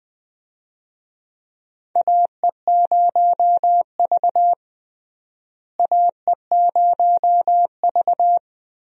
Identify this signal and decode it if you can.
Morse code, heard as a keyed tone, about 20 wpm, 700 Hz. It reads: AE0V AE0V
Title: USA Call Signs - 4 Characters - Repeated 3x - 20wpm